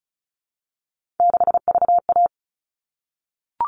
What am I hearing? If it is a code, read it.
Morse code, heard as a keyed tone, about 34 wpm, 700 Hz. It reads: 6VA
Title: All Cut Numbers - 3-Digit 35wpm